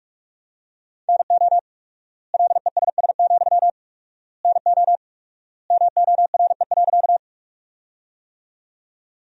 Morse code, heard as a keyed tone, about 46 wpm, 700 Hz. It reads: NO LESS, NO MORE.
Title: 4x Word-Spacing - Sentences from Top 400 Words - 45wpm